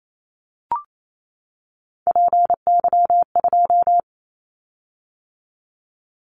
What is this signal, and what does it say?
Morse code, heard as a keyed tone, about 28 wpm, 700 Hz. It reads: PY2
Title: Callsign Prefixes for Top 100 DXCC - 28wpm